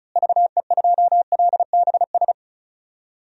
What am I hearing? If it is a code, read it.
Morse code, heard as a keyed tone, about 36 wpm, 700 Hz. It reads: VE2LBS
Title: DXCC-100 Call Signs - 6 Characters - Repeated 2x - 35wpm